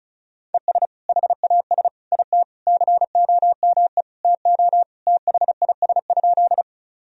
Morse code, heard as a keyed tone, about 34 wpm, 700 Hz. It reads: ES HAS IT COME TO THIS?
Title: Sentences from Top 100 Words and Abbreviations 35wpm